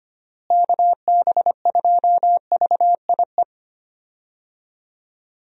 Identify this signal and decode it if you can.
Morse code, heard as a keyed tone, about 25 wpm, 700 Hz. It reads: KB2VIE
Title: USA Call Signs - 6 Characters - Repeated 2x - 25wpm